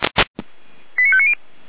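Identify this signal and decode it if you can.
Five-tone selective call (ZVEI), heard as quick runs of five tones, tones of about 70 ms. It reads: 89590
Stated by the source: ZVEI five-tone selective call received on an SDR by Feuerwehr Orlamünde.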